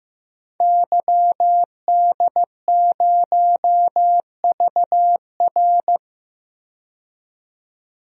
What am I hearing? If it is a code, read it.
Morse code, heard as a keyed tone, about 15 wpm, 700 Hz. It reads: YD0VR
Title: DXCC-100 Call Signs - 5 Characters - 15wpm